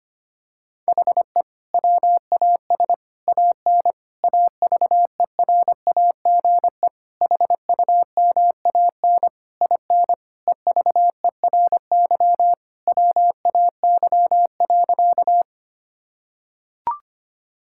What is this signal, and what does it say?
Morse code, heard as a keyed tone, about 25 wpm, 700 Hz. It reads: HE WAS AN AVERAGE HUMAN IN EVERY WAY.
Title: Sentences from Top 1000 Words - Review Part 6 - 25wpm